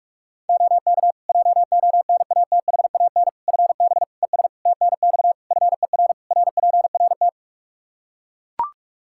Morse code, heard as a keyed tone, about 41 wpm, 700 Hz. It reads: OK JONATHAN FB ES TNX FER RPRT
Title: Practice QSO - Exchange 2 - Info - 45wpm